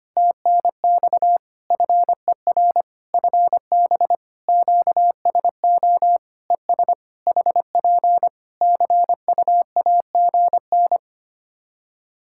Morse code, heard as a keyed tone, about 25 wpm, 700 Hz. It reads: TNX FER FB QSO ES HP CUAGN
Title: K1USN SST QSO Practice 25wpm